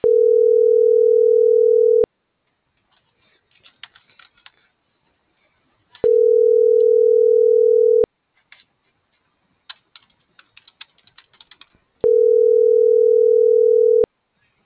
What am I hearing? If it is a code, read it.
Ringback tone (North America), 440 + 480 Hz.
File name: CENTRAL TERRESTRE / out els..laz.elo.mo.24